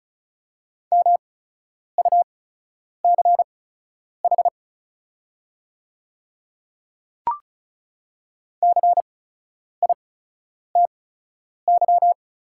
Morse code, heard as a keyed tone, about 36 wpm, spaced as for 10 wpm, 690 Hz. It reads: MUCH CITY
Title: 8x Character Spacing - Top 300 Words - 35wpm